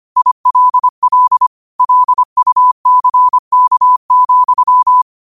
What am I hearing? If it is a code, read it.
Morse code, heard as a keyed tone, about 25 wpm, 1000 Hz. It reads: ILL LUCK,